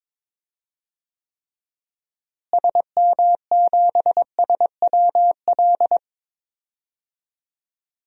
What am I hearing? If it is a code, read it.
Morse code, heard as a keyed tone, about 22 wpm, 700 Hz. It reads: SM7SWL
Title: DXCC-100 Call Signs - 6 Characters - Repeated 2x - 22wpm